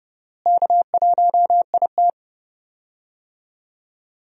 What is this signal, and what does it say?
Morse code, heard as a keyed tone, about 30 wpm, 700 Hz. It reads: K1IT
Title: USA Call Signs - 4 Characters - Repeated 2x - 30wpm